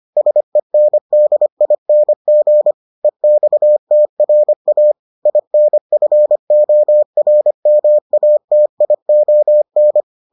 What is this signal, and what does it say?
Morse code, heard as a keyed tone, about 25 wpm, 600 Hz. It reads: SENDING EXTRA INFORMATION